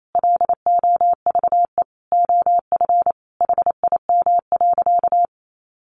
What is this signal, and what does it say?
Morse code, heard as a keyed tone, about 28 wpm, 700 Hz. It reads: LOVE OF HIM.
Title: Sentences from Top 1500 Words 28wpm